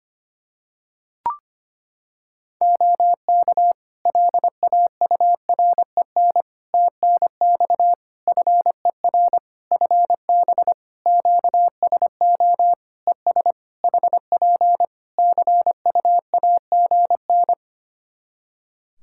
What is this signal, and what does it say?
Morse code, heard as a keyed tone, about 25 wpm, 700 Hz. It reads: OK LAUREN TNX FER FB QSO ES HP CUAGN
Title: Practice QSO - Exchange 4 - Ending - 25wpm